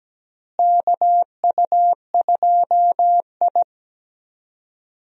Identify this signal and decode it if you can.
Morse code, heard as a keyed tone, about 17 wpm, 700 Hz. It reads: KU2I
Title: USA Call Signs - 4 Characters - Repeated 2x - 17wpm